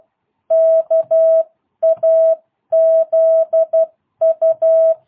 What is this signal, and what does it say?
Morse code, heard as a keyed tone, about 12 wpm, 650 Hz. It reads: KAZU